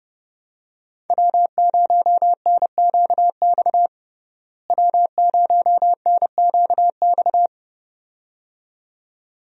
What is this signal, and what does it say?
Morse code, heard as a keyed tone, about 30 wpm, 700 Hz. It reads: W0NQX W0NQX
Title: USA Call Signs - 5 Characters - Repeated 2x - 30wpm